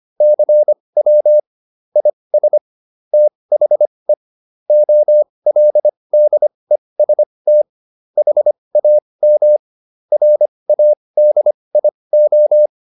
Morse code, heard as a keyed tone, about 25 wpm, spaced as for 20 wpm, 600 Hz. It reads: CW IS THE OLDEST HAM RADIO